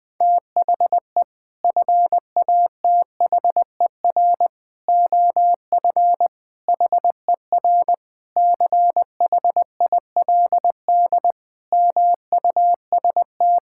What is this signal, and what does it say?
Morse code, heard as a keyed tone, about 21 wpm, 700 Hz. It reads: THE FATHER OF HER CHILD MUST
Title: Sentences from Top 1000 Words - Review Part 5 - 20wpm